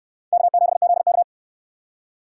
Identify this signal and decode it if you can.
Morse code, heard as a keyed tone, about 69 wpm, 700 Hz. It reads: K7ZK